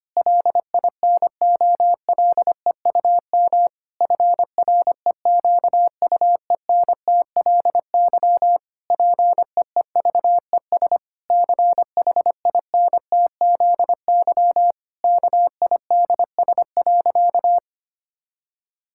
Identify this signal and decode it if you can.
Morse code, heard as a keyed tone, about 25 wpm, 700 Hz. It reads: LINOLEUM FREQUENTLY PEEVES CHINTZY KIDS.
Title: Pangram Challenge v2 - 25wpm